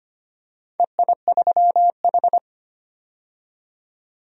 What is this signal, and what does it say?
Morse code, heard as a keyed tone, about 25 wpm, 700 Hz. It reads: EI3H
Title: DXCC-100 Call Signs - 4 Characters - 25wpm